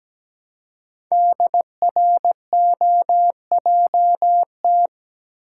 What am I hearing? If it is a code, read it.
Morse code, heard as a keyed tone, about 17 wpm, 700 Hz. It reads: DROJT
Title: Code Groups - 5 Letters - 17wpm